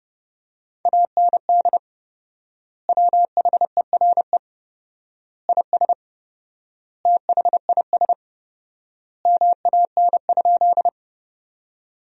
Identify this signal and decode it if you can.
Morse code, heard as a keyed tone, about 30 wpm, 700 Hz. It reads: AND WHERE IS THIS MAN?
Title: 4x Word-Spacing - Sentences from Top 200 Words - 30wpm